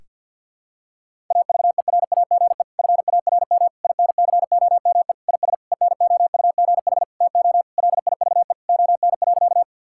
Morse code, heard as a keyed tone, about 52 wpm, 700 Hz. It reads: AVERAGE FARM INCOME IS ENOUGH TO LIVE ON.
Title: Sentences from Top 1000 Words - Review Part 6 - 50wpm